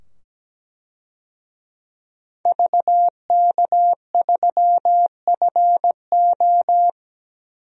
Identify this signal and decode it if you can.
Morse code, heard as a keyed tone, about 17 wpm, 700 Hz. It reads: VK3FO
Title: DXCC-100 Call Signs - 5 Characters - 17wpm